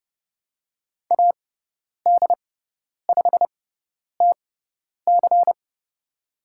Morse code, heard as a keyed tone, about 30 wpm, spaced as for 10 wpm, 700 Hz. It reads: AD5TC
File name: Lesson 16g - Letter c - Call Signs 30wpm